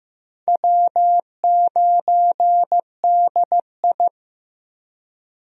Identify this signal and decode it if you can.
Morse code, heard as a keyed tone, about 15 wpm, 700 Hz. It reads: W9DI
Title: USA Call Signs - 4 Characters - Repeated 3x - 15wpm